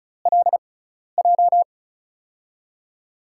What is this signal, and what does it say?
Morse code, heard as a keyed tone, about 36 wpm, spaced as for 12 wpm, 700 Hz. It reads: LJ
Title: USA Call Signs - 6 Characters - 6x Character Spacing - 35wpm